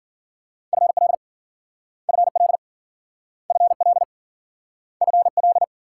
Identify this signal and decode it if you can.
Morse code, heard as a keyed tone, about 53 wpm, 700 Hz. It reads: FL FL FL FL